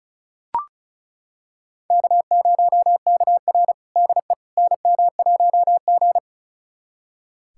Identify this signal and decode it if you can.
Morse code, heard as a keyed tone, about 36 wpm, 700 Hz. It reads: K0KR DE NM1G
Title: USA Callsign DE Callsign - 4 Characters - 35wpm